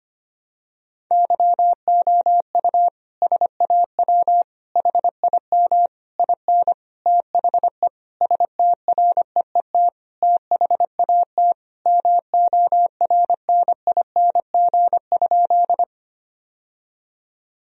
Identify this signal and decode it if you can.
Morse code, heard as a keyed tone, about 25 wpm, 700 Hz. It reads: YOU SAW HIM IN THE STREET THAT MORNING?
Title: Sentences from Top 1000 Words - Review Part 3 - 25wpm